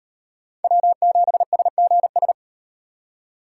Morse code, heard as a keyed tone, about 38 wpm, 700 Hz. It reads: W7SGS